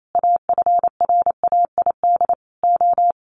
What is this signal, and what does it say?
Morse code, heard as a keyed tone, about 28 wpm, 700 Hz. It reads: AFRAID O
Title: Sentences from Top 1500 Words 28wpm